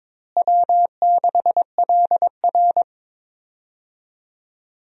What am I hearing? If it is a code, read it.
Morse code, heard as a keyed tone, about 22 wpm, 700 Hz. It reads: W6LR
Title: USA Call Signs - 4 Characters - Repeated 3x - 22wpm